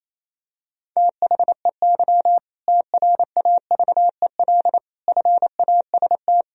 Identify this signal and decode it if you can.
Morse code, heard as a keyed tone, about 28 wpm, 700 Hz. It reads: THEY TRAVEL FAST
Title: Sentences from Top 1000 Words - Review Part 3 - 28wpm